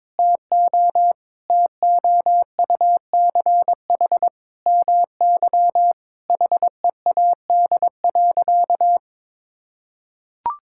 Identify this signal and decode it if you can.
Morse code, heard as a keyed tone, about 22 wpm, 700 Hz. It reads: TO TOUCH MY HEAD.
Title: Sentences from Top 1600 Words 22wpm